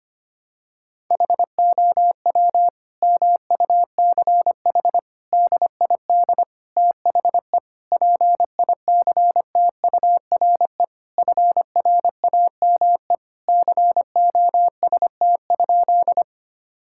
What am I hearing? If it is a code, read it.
Morse code, heard as a keyed tone, about 25 wpm, 700 Hz. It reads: HOW MUCH DID THE PICTURE FRAME COST?
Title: Sentences from Top 1000 Words - Review Part 6 - 25wpm